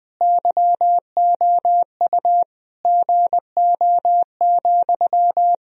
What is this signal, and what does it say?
Morse code, heard as a keyed tone, about 21 wpm, 700 Hz. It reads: YOU GO,
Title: Sentences from Top 1000 Words - Review Part 1 - 20wpm